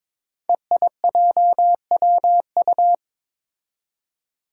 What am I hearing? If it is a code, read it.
Morse code, heard as a keyed tone, about 23 wpm, 700 Hz. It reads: EIJWU
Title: Five Challenging Letter-Number Combinations 22wpm